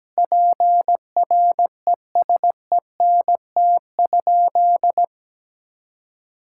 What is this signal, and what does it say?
Morse code, heard as a keyed tone, about 17 wpm, 700 Hz. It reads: PRESENT?
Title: Sentences from Top 1000 Words - Review Part 5 - 17wpm